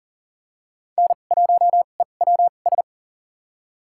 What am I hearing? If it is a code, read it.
Morse code, heard as a keyed tone, about 39 wpm, spaced as for 29 wpm, 700 Hz. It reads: N1EWS
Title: USA Call Signs - 5 Characters - 2x Character Spacing - 40wpm